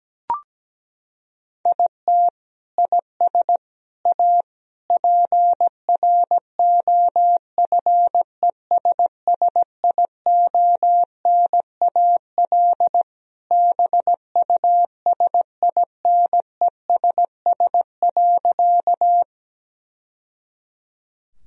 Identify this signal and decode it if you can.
Morse code, heard as a keyed tone, about 17 wpm, 700 Hz. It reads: IT IS A PROFESSIONAL BUSINESS.
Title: Sentences from Top 1000 Words - Review Part 4 - 17wpm